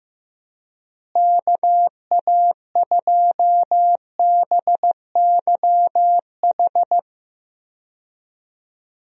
Morse code, heard as a keyed tone, about 15 wpm, 700 Hz. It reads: KA2BYH
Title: USA Call Signs - 6 Characters - Repeated 2x - 15wpm